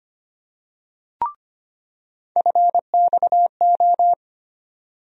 Morse code, heard as a keyed tone, about 25 wpm, 700 Hz. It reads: FXO
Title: USA Call Sign Postfixes - 3 Letters 25wpm